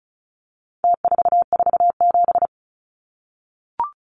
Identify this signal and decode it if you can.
Morse code, heard as a keyed tone, about 34 wpm, 700 Hz. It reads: T447
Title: Numbers - Contest Style - 4-Digit 35wpm